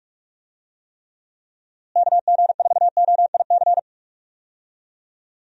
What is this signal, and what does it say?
Morse code, heard as a keyed tone, about 42 wpm, 700 Hz. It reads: KG4OIC
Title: USA Call Signs - 6 Characters - Repeated 2x - 45wpm